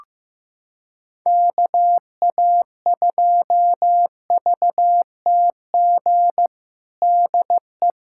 Morse code, heard as a keyed tone, about 15 wpm, 700 Hz. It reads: KA2VTG DE
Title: USA Callsign DE Callsign - 6 Characters - 15wpm